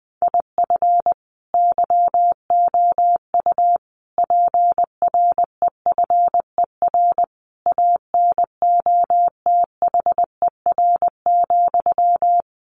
Morse code, heard as a keyed tone, about 19 wpm, 700 Hz. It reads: IF YOU PREFER ANOTHER,
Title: Sentences from Top 1100 Words 20wpm